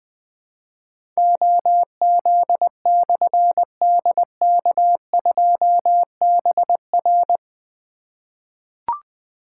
Morse code, heard as a keyed tone, about 20 wpm, 700 Hz. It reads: OZ/DK2BR
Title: Difficult Call Signs 20wpm